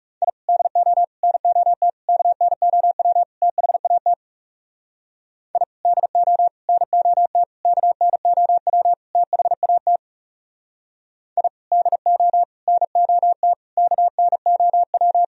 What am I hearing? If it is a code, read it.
Morse code, heard as a keyed tone, about 39 wpm, 700 Hz. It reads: I DO NOT KNOW THAT I DO NOT KNOW THAT I DO NOT KNOW